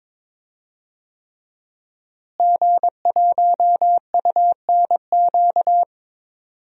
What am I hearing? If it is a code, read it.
Morse code, heard as a keyed tone, about 22 wpm, 700 Hz. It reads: G1UNQ
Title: DXCC-100 Call Signs - 5 Characters - Repeated 3x - 22wpm